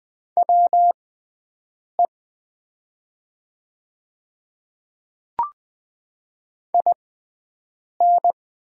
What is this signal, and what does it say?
Morse code, heard as a keyed tone, about 20 wpm, spaced as for 7 wpm, 700 Hz. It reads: WE IN
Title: 6x Character Spacing - Top 200 Words - 20wpm